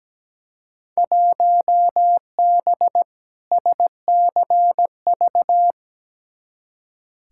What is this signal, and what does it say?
Morse code, heard as a keyed tone, about 17 wpm, 700 Hz. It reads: 1B SCV
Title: Field Day Exchanges - 17wpm